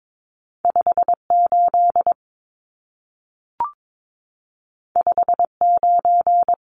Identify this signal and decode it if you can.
Morse code, heard as a keyed tone, about 22 wpm, 700 Hz. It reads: 58 59
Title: Numbers - 2-Digit 22wpm